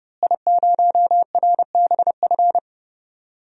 Morse code, heard as a keyed tone, about 30 wpm, 700 Hz. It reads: I0RBF
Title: DXCC-100 Call Signs - 5 Characters - Repeated 2x - 30wpm